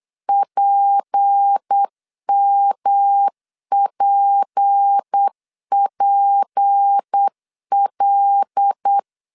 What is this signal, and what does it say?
Morse code, heard as a keyed tone, about 8 wpm, 800 Hz. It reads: PMPPL
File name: LEZIONI / 6 PKX0 / 6 P 8wpm 3025